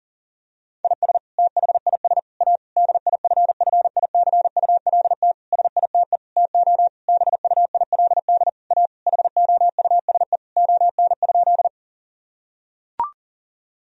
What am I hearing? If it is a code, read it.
Morse code, heard as a keyed tone, about 40 wpm, 700 Hz. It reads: IS THIS A DIFFICULT SITE TO BUILD A HOUSE ON?
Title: Sentences from Top 1000 Words - Review Part 4 - 40wpm